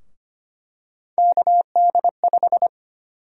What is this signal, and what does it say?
Morse code, heard as a keyed tone, about 25 wpm, 700 Hz. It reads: KD5
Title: USA Call Sign Prefixes - 25wpm